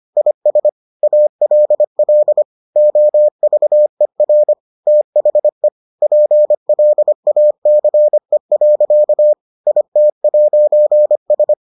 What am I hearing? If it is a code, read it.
Morse code, heard as a keyed tone, about 25 wpm, 600 Hz. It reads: IS ALL OVER THE PLACE. IT'S